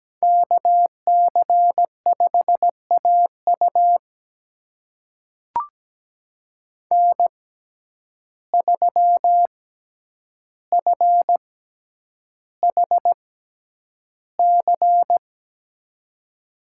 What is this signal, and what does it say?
Morse code, heard as a keyed tone, about 17 wpm, spaced as for 9 wpm, 700 Hz. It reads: KC5AU N3FHC
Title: USA Call Signs - 5 Characters - 6x Character Spacing - 17wpm